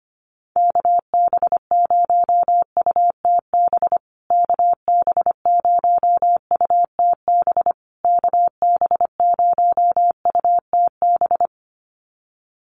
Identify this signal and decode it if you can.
Morse code, heard as a keyed tone, about 25 wpm, 700 Hz. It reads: KB0UTB KB0UTB KB0UTB
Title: USA Call Signs Repeated 3x 25wpm